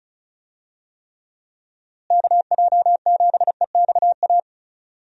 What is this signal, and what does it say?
Morse code, heard as a keyed tone, about 34 wpm, 700 Hz. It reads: KJ7EXA
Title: USA Call Signs - 6 Characters - Repeated 2x - 35wpm